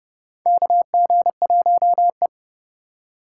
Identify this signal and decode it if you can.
Morse code, heard as a keyed tone, about 30 wpm, 700 Hz. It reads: KG1E
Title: USA Call Signs - 4 Characters - 3x Character Spacing - 30wpm